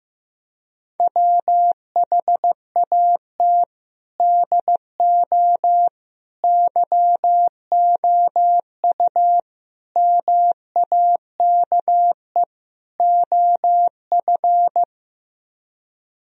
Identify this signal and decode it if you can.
Morse code, heard as a keyed tone, about 15 wpm, 700 Hz. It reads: WHAT DO YOU MAKE OF
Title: Speed-Racing - Sets of 5 Words 15wpm